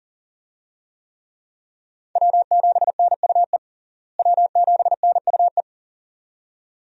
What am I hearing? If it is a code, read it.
Morse code, heard as a keyed tone, about 40 wpm, 700 Hz. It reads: W7NUE W7NUE
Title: USA Call Signs - 5 Characters - Repeated 2x - 40wpm